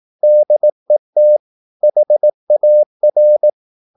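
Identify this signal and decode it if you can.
Morse code, heard as a keyed tone, about 18 wpm, 600 Hz. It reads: DET HAR
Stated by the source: Slow (QRS) Morse practice at 18 wpm, CW med Gnister.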